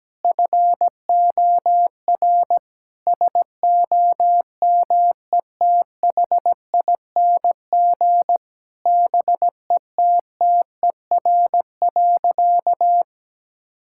Morse code, heard as a keyed tone, about 17 wpm, 700 Hz. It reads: FOR SOMETHING BETTER.
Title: Sentences from Top 1000 Words - Review Part 6 - 17wpm